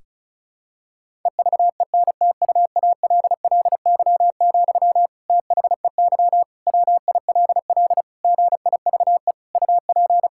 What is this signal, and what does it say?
Morse code, heard as a keyed tone, about 34 wpm, 700 Hz. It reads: EVENTUALLY, THEY WILL GIVE UP
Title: Sentences from Top 1000 Words - Review Part 6 - 35wpm